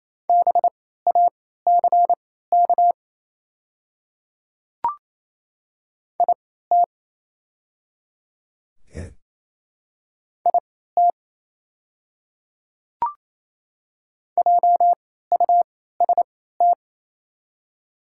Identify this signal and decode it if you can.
Morse code, heard as a keyed tone, about 28 wpm, spaced as for 16 wpm, 690 Hz. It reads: BACK IT IT JUST
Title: 3x Character Spacing - Top 500 Words - 28wpm